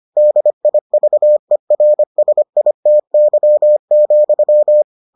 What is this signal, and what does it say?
Morse code, heard as a keyed tone, about 25 wpm, 600 Hz. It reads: DIVERSITY,